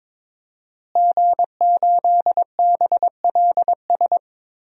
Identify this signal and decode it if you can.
Morse code, heard as a keyed tone, about 22 wpm, 700 Hz. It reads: G8BLS
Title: DXCC-100 Call Signs - 5 Characters - Repeated 2x - 22wpm